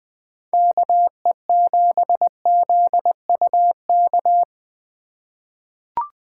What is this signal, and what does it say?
Morse code, heard as a keyed tone, about 20 wpm, 700 Hz. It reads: KE7ZUK
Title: USA Call Signs - 6 Characters - 3x Character Spacing - 20wpm